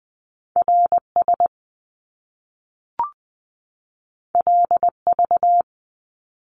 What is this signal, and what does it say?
Morse code, heard as a keyed tone, about 20 wpm, 700 Hz. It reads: R S L V
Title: Top Two Letter Combinations in Words 20wpm